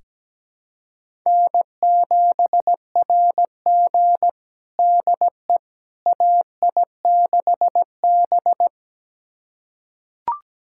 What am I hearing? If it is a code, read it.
Morse code, heard as a keyed tone, about 17 wpm, 700 Hz. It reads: N7RG DE AI6B
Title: USA Callsign DE Callsign - 4 Characters - 17wpm